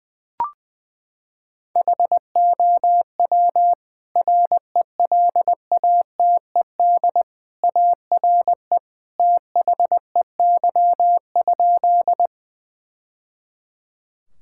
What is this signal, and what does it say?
Morse code, heard as a keyed tone, about 20 wpm, 700 Hz. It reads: HOW RELATED ARE THEY?
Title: Sentences from Top 1000 Words - Review Part 5 - 20wpm